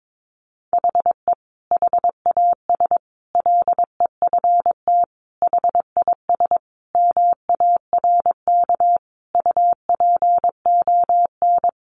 Morse code, heard as a keyed tone, about 22 wpm, 700 Hz. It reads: HE HAS LEFT HIS MARK UPON
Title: Sentences from Top 1100 Words 22wpm